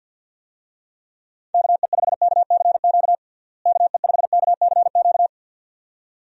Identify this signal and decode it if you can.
Morse code, heard as a keyed tone, about 50 wpm, 700 Hz. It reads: KE5KCQ KE5KCQ
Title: USA Call Signs - 6 Characters - Repeated 2x - 50wpm